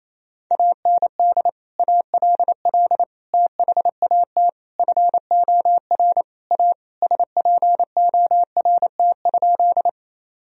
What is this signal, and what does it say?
Morse code, heard as a keyed tone, about 28 wpm, 700 Hz. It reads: AND ALL THAT FOR A SPORT?
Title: Sentences from Top 1000 Words - Review Part 4 - 28wpm